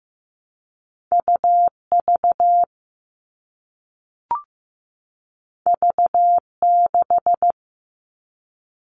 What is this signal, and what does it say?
Morse code, heard as a keyed tone, about 15 wpm, 700 Hz. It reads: UV V6
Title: All Cut Numbers - 2-Digit 15wpm